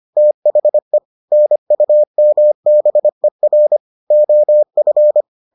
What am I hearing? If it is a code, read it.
Morse code, heard as a keyed tone, about 25 wpm, 600 Hz. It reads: THE NUMBER OF